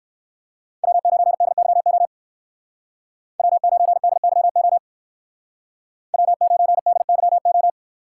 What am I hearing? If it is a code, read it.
Morse code, heard as a keyed tone, about 55 wpm, 700 Hz. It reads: W9DYO W9DYO W9DYO
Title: Speed Racing - USA 1x3 Call Signs 45wpm